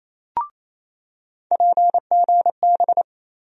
Morse code, heard as a keyed tone, about 27 wpm, 700 Hz. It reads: PGB
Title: USA Call Sign Postfixes - 3 Letters 28wpm